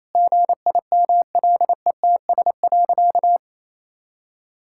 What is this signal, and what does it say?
Morse code, heard as a keyed tone, about 28 wpm, 700 Hz. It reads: GIMLETS.
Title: Pangram Challenge v2 - 28wpm